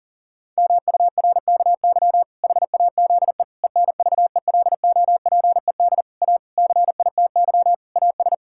Morse code, heard as a keyed tone, about 40 wpm, 700 Hz. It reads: MURKY HAZE ENVELOPED A CITY AS
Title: Pangram Challenge v2 - 40wpm